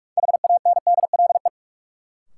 Morse code, heard as a keyed tone, about 46 wpm, 700 Hz. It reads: HANDLE
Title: Sentences from Top 1100 Words - New Word Drill - 45wpm